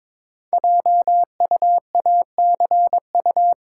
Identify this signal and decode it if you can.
Morse code, heard as a keyed tone, about 22 wpm, 700 Hz. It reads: JUACU
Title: Code Groups - 5 Letters - 22wpm